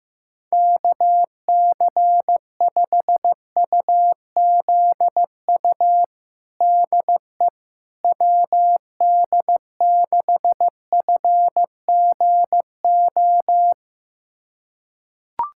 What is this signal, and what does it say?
Morse code, heard as a keyed tone, about 15 wpm, 700 Hz. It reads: KC5UZU DE WD6FGO
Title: USA Callsign DE Callsign 15wpm